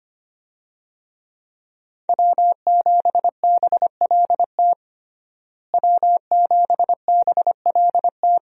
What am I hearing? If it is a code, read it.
Morse code, heard as a keyed tone, about 25 wpm, 700 Hz. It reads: W7BLT W7BLT
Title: USA Call Signs - 5 Characters - Repeated 2x - 25wpm